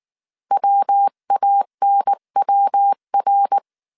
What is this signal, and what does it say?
Morse code, heard as a keyed tone, about 19 wpm, 800 Hz. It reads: WANWR